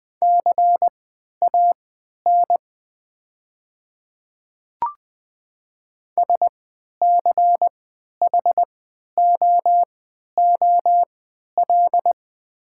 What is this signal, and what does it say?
Morse code, heard as a keyed tone, about 20 wpm, spaced as for 11 wpm, 700 Hz. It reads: CAN SCHOOL
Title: 3x Character Spacing - Top 500 Words - 20wpm